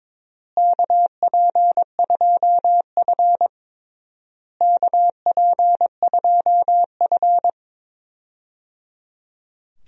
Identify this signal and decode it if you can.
Morse code, heard as a keyed tone, about 22 wpm, 700 Hz. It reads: KP2F KP2F
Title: DXCC-100 Call Signs - 4 Characters - Repeated 3x - 22wpm